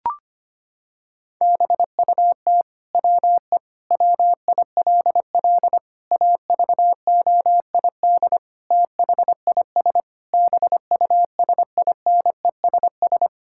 Morse code, heard as a keyed tone, about 25 wpm, 700 Hz. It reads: BUT WE WILL AVOID THIS BUSINESS
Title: Sentences from Top 1000 Words - Review Part 6 - 25wpm